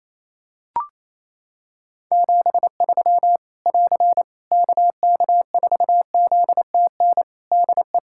Morse code, heard as a keyed tone, about 28 wpm, 700 Hz. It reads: 73 <AR> KK4ZTN DE
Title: Practice QSO - Exchange 4 - Ending - 28wpm